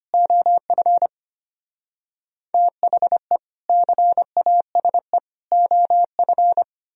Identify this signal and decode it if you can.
Morse code, heard as a keyed tone, about 31 wpm, 700 Hz. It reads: OF THE CASE OF
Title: Speed-Racing - Sets of 3 Words 25wpm